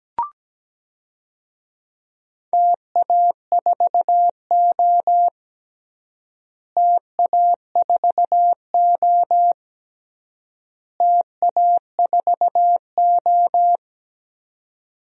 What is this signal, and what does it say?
Morse code, heard as a keyed tone, about 17 wpm, 700 Hz. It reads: TA4O TA4O TA4O
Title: DXCC-100 Call Signs - 4 Characters - Repeated 3x - 17wpm